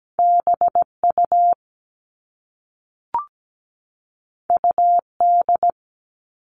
Morse code, heard as a keyed tone, about 17 wpm, 700 Hz. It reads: BU UD
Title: All Cut Numbers - 2-Digit 17wpm